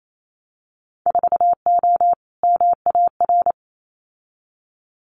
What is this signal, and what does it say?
Morse code, heard as a keyed tone, about 28 wpm, 700 Hz. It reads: 4O MAR
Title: Winter Field Day Exchanges Encore 28wpm